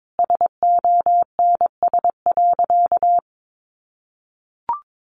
Morse code, heard as a keyed tone, about 22 wpm, 700 Hz. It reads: SONS.
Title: A Sound Of Thunder 22wpm